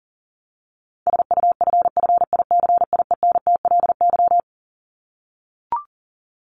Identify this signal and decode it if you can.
Morse code, heard as a keyed tone, about 40 wpm, 700 Hz. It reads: SUFFICIENTLY
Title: LY Suffix - Top 100 Words with Base Words 40wpm